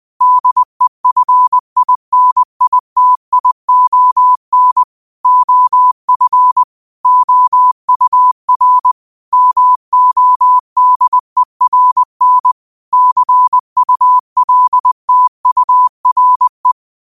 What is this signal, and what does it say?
Morse code, heard as a keyed tone, about 20 wpm, 1000 Hz. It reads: DEFINITION OF OUR MODERN CULTURE
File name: qotdfiles / 20wpm